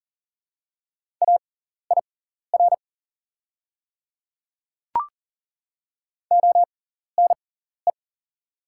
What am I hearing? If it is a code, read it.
Morse code, heard as a keyed tone, about 40 wpm, spaced as for 14 wpm, 700 Hz. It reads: AIR ONE
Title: 6x Character Spacing - Top 200 Words - 40wpm